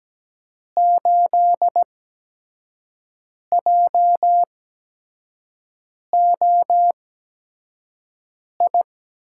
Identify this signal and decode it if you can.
Morse code, heard as a keyed tone, about 17 wpm, spaced as for 5 wpm, 700 Hz. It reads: 8JOI
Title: USA Call Signs - 6 Characters - 8x Character Spacing - 17wpm